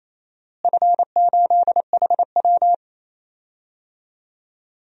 Morse code, heard as a keyed tone, about 28 wpm, 700 Hz. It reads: F8HW
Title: DXCC-100 Call Signs - 4 Characters - Repeated 3x - 28wpm